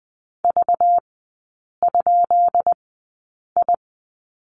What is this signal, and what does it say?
Morse code, heard as a keyed tone, about 20 wpm, spaced as for 8 wpm, 690 Hz. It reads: V?I
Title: 191 Lesson 31e - Letter v - Three Characters 20wpm